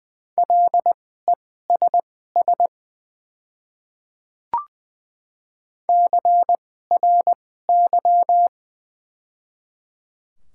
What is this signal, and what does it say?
Morse code, heard as a keyed tone, about 20 wpm, spaced as for 14 wpm, 690 Hz. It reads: LESS CRY
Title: 2x Character Spacing - Top 500 Words - 20wpm